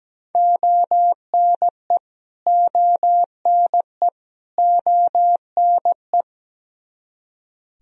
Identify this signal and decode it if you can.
Morse code, heard as a keyed tone, about 17 wpm, 700 Hz. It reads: ONE ONE ONE
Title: Top 100 Words Repeated 17wpm